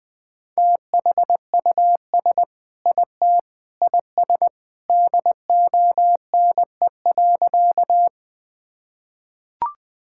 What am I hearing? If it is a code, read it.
Morse code, heard as a keyed tone, about 19 wpm, 700 Hz. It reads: THUS IT IS DONE.
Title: Sentences from Top 1000 Words - Review Part 5 - 20wpm